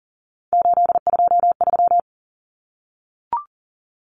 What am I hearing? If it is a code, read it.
Morse code, heard as a keyed tone, about 40 wpm, 700 Hz. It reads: 823
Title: Common Cut Numbers - 3-Digit 40wpm